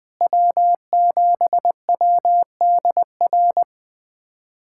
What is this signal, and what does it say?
Morse code, heard as a keyed tone, about 20 wpm, 700 Hz. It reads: W7WDR
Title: USA Callsign DE Callsign - 5 Characters - 20wpm